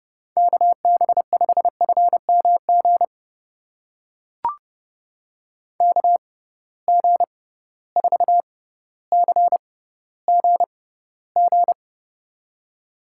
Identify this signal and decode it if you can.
Morse code, heard as a keyed tone, about 30 wpm, spaced as for 15 wpm, 700 Hz. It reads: KB5FMG KG4CGG
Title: USA Call Signs - 6x Character Spacing - 30wpm